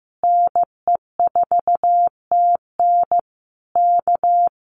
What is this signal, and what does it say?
Morse code, heard as a keyed tone, about 15 wpm, 700 Hz. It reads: NE4TN K
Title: Common QSO Phrases Encore 15wpm